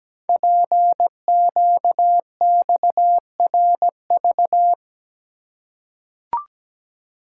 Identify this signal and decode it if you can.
Morse code, heard as a keyed tone, about 17 wpm, 700 Hz. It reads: PQXRV
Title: Code Groups - 5 Letters - 17wpm